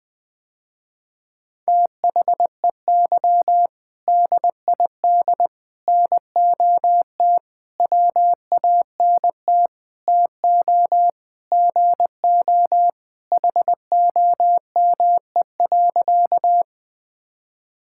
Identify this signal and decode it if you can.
Morse code, heard as a keyed tone, about 20 wpm, 700 Hz. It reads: THEY DID NOT WANT TO GO HOME.
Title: Sentences with 5-Word Phrases - Spoken Beginning - 20wpm